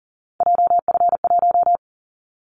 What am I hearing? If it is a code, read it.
Morse code, heard as a keyed tone, about 40 wpm, 700 Hz. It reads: JF1
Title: Callsign Prefixes for Top 25 DXCC - Repeated 3x - 40wpm